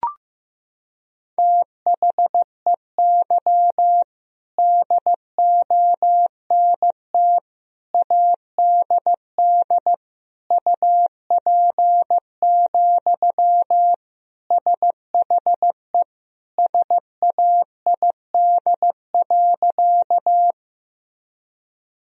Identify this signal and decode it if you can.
Morse code, heard as a keyed tone, about 16 wpm, 700 Hz. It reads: THEY DONT ADD UP, SHE SAID.
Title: Sentences from Top 1000 Words - Review Part 1 - 15wpm